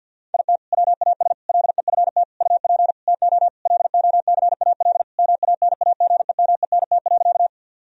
Morse code, heard as a keyed tone, about 50 wpm, 700 Hz. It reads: IT WAS LEFT UP TO LOCAL MANAGEMENT.
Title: Sentences from Top 1000 Words - Review Part 4 - 50wpm